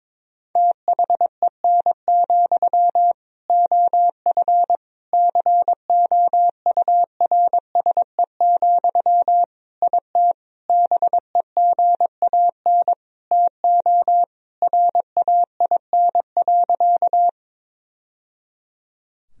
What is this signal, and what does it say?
Morse code, heard as a keyed tone, about 22 wpm, 700 Hz. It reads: THEN, OF COURSE, IT BEGAN TO RAIN.
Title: Sentences from Top 1000 Words - Review Part 1 - 22wpm